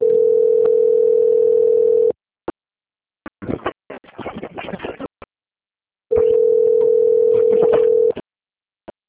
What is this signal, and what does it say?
Ringback tone (North America), 440 + 480 Hz.
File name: out- - - - - .189